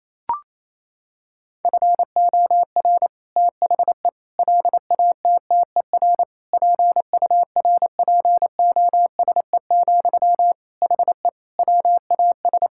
Morse code, heard as a keyed tone, about 28 wpm, 700 Hz. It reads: FOR THE LATTER PURPOSE, HE WAS
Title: Sentences from Top 1600 Words 28wpm